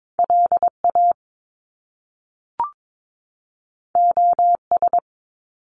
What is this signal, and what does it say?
Morse code, heard as a keyed tone, about 22 wpm, 700 Hz. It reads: LA OS
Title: Top Two Letter Combinations in Words 22wpm